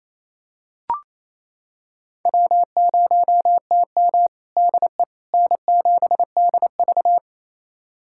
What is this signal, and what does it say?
Morse code, heard as a keyed tone, about 28 wpm, 700 Hz. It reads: W0TM DE N7DV
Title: USA Callsign DE Callsign - 4 Characters - 28wpm